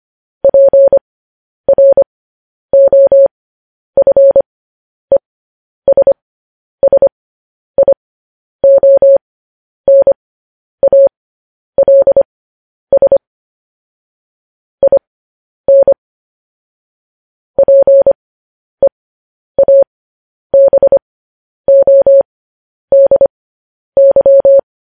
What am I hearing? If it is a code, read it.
Morse code, heard as a keyed tone, about 25 wpm, spaced as for 10 wpm, 550 Hz. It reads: PROFESSIONALS IN PEABODY
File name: podcast / News headlines-1x-25x10